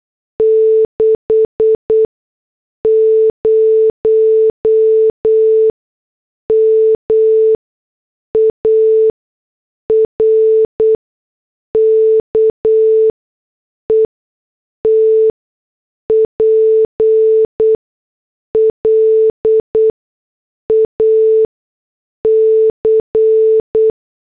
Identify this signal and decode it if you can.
Morse code, heard as a keyed tone, about 8 wpm, spaced as for 6 wpm, 440 Hz. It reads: 60 MARKETPLAC